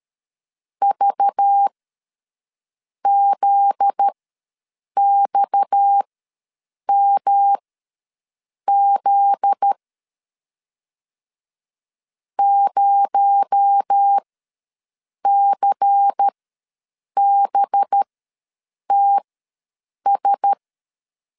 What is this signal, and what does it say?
Morse code, heard as a keyed tone, about 13 wpm, spaced as for 6 wpm, 800 Hz. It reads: VZXMZ 0CBTS
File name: LEZIONI / 7 GF9Z / 7 GF9Z ws 12wpm 3321